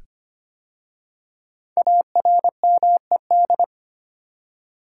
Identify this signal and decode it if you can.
Morse code, heard as a keyed tone, about 25 wpm, 700 Hz. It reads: ARMED
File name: 5 Letter Words - First 3 Letters Given - 25wpm